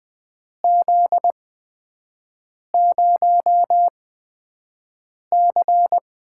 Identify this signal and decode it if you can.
Morse code, heard as a keyed tone, about 20 wpm, spaced as for 5 wpm, 700 Hz. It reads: Z0C
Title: USA Call Signs - 4 Characters - 8x Character Spacing - 20wpm